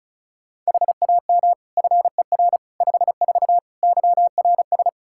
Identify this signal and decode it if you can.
Morse code, heard as a keyed tone, about 34 wpm, 700 Hz. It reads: HAM FER 54 YRS
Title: K1USN SST QSO Practice 35wpm